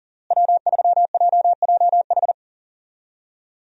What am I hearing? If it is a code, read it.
Morse code, heard as a keyed tone, about 40 wpm, 700 Hz. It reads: W3JJH
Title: USA Call Signs - 5 Characters - Repeated 2x - 40wpm